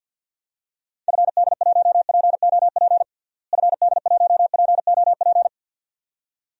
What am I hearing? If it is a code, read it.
Morse code, heard as a keyed tone, about 53 wpm, 700 Hz. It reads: FD1POP FD1POP
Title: DXCC-100 Call Signs - 6 Characters - Repeated 2x - 50wpm